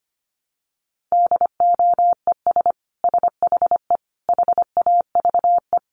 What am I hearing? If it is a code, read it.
Morse code, heard as a keyed tone, about 25 wpm, 700 Hz. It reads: DOES SHE HAVE
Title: Sentences from Top 800 Words 25wpm